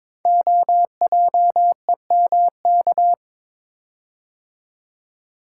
Morse code, heard as a keyed tone, about 22 wpm, 700 Hz. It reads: OJEMK
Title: Code Groups - 5 Letters - 22wpm